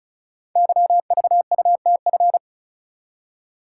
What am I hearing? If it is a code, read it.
Morse code, heard as a keyed tone, about 36 wpm, 700 Hz. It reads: YVUTF
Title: Code Groups - 5 Letter:Numbers - 35wpm